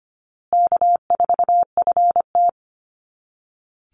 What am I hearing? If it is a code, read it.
Morse code, heard as a keyed tone, about 25 wpm, 700 Hz. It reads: K4FT
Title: Call Signs Repeated 3x - 25wpm